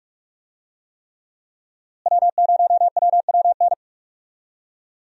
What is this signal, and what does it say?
Morse code, heard as a keyed tone, about 46 wpm, 700 Hz. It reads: W0WWN
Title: USA Call Signs - 5 Characters - Repeated 3x - 45wpm